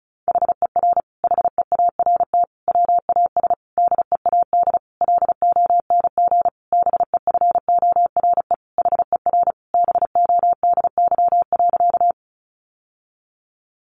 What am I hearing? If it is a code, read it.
Morse code, heard as a keyed tone, about 34 wpm, 700 Hz. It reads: HER HEART WAS DEAD LONG BEFORE HER BODY.
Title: Sentences from Top 1500 Words 35wpm